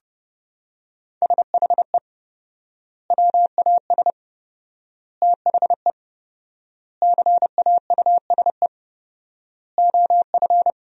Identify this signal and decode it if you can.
Morse code, heard as a keyed tone, about 30 wpm, 700 Hz. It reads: SHE WAS THE CAUSE OF
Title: 4x Word-Spacing - Sentences from Top 200 Words - 30wpm